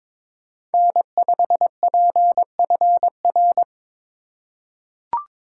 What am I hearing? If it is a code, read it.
Morse code, heard as a keyed tone, about 22 wpm, 700 Hz. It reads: N5PFR
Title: USA Call Signs - 5 Characters - 4x Character Spacing - 22wpm